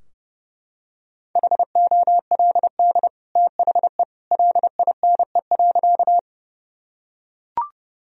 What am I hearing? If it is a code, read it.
Morse code, heard as a keyed tone, about 32 wpm, 700 Hz. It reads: HOLD THE LINE.
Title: Sentences from Top 1000 Words - Review Part 2 - 30wpm